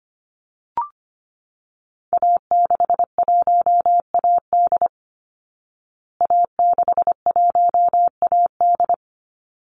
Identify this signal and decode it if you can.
Morse code, heard as a keyed tone, about 25 wpm, 700 Hz. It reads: A61AD A61AD
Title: Call Signs Repeated 3x - 25wpm